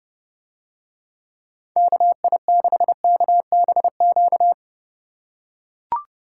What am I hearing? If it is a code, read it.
Morse code, heard as a keyed tone, about 31 wpm, 700 Hz. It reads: KI6KBQ